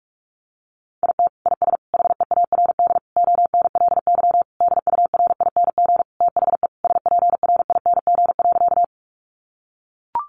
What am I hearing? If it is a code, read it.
Morse code, heard as a keyed tone, about 44 wpm, 700 Hz. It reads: IT IS HEARD ONLY DURING THE SPRING.
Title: Sentences from Top 1200 Words 45wpm